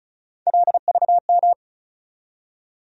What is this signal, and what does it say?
Morse code, heard as a keyed tone, about 34 wpm, 700 Hz. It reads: LVM
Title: USA Call Sign Postfixes - 3 Letters 35wpm